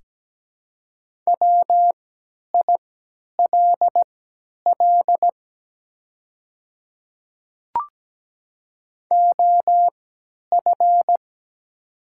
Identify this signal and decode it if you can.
Morse code, heard as a keyed tone, about 17 wpm, spaced as for 10 wpm, 700 Hz. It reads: WILL OF
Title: 3x Character Spacing - Top 100 Words - 17wpm